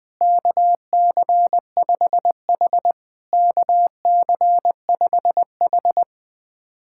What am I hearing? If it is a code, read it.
Morse code, heard as a keyed tone, about 20 wpm, 700 Hz. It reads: KC5H KC5H
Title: POTA QSO Practice - 20wpm